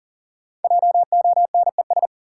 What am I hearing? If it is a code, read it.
Morse code, heard as a keyed tone, about 37 wpm, 700 Hz. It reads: JONES
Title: Sets of 5 Words - New Word Drill - 40wpm